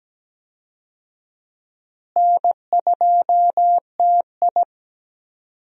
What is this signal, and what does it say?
Morse code, heard as a keyed tone, about 17 wpm, 700 Hz. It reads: N2TI
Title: USA Call Signs - 4 Characters - Repeated 2x - 17wpm